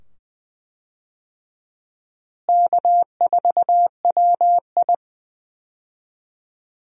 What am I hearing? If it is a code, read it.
Morse code, heard as a keyed tone, about 20 wpm, 700 Hz. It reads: K4WI
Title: USA Call Signs - 4 Characters - Repeated 2x - 20wpm